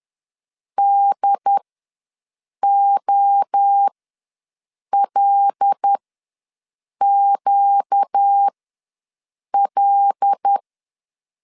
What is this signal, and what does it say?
Morse code, heard as a keyed tone, about 10 wpm, spaced as for 6 wpm, 800 Hz. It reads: DOLQL
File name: LEZIONI / 5 LBVC / 5 L ws 10wpm 3636